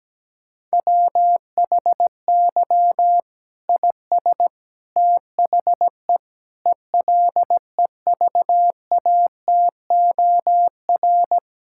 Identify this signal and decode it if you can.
Morse code, heard as a keyed tone, about 17 wpm, 700 Hz. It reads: WHY IS THE ELEVATOR
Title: Sentences from Top 1600 Words 17wpm